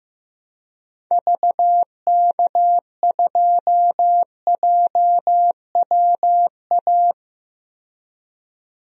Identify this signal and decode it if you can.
Morse code, heard as a keyed tone, about 15 wpm, 700 Hz. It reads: VK2JWA
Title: DXCC-100 Call Signs - 6 Characters - 15wpm